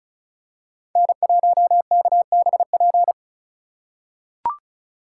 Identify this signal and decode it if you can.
Morse code, heard as a keyed tone, about 34 wpm, 700 Hz. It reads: N1KBP
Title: USA Call Signs - 2x Character Spacing - 35wpm